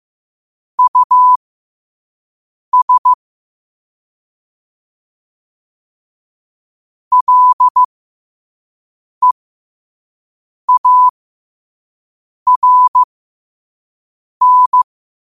Morse code, heard as a keyed tone, about 15 wpm, spaced as for 5 wpm, 1000 Hz. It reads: US LEARN
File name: qotdfiles / 5wpm